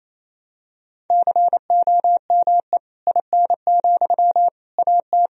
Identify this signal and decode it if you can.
Morse code, heard as a keyed tone, about 27 wpm, 700 Hz. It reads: COME IN, AT
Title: Sentences from Top 1000 Words - Review Part 4 - 28wpm